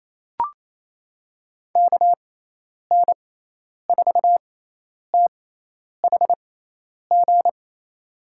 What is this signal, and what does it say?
Morse code, heard as a keyed tone, about 28 wpm, spaced as for 10 wpm, 700 Hz. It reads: KN4THG
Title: USA Call Signs - 6x Character Spacing - 28wpm